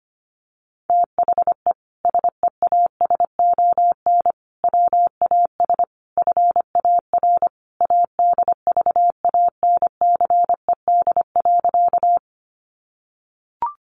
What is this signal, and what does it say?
Morse code, heard as a keyed tone, about 25 wpm, 700 Hz. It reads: THE SEASON WAS FAR ADVANCED.
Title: Sentences from Top 1400 Words 25wpm